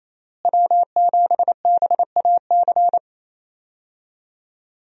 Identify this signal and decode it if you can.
Morse code, heard as a keyed tone, about 28 wpm, 700 Hz. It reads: W7BAC
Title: USA Call Signs - 5 Characters - Repeated 3x - 28wpm